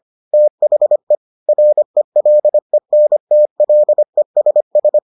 Morse code, heard as a keyed tone, about 25 wpm, 600 Hz. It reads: THE RELENTLESS